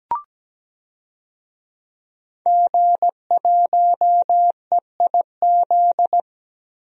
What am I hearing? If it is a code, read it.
Morse code, heard as a keyed tone, about 17 wpm, 700 Hz. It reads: G1EIZ
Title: DXCC-100 Call Signs - 5 Characters - 17wpm